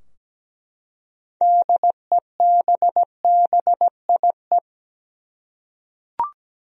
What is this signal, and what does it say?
Morse code, heard as a keyed tone, about 17 wpm, 700 Hz. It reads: DEBBIE
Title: USA Common Names - 17wpm